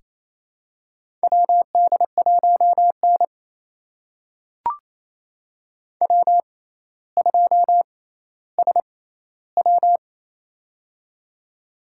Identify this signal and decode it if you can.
Morse code, heard as a keyed tone, about 28 wpm, spaced as for 14 wpm, 700 Hz. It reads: WD1N W2SW
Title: USA Call Signs - 4 Characters - 6x Character Spacing - 28wpm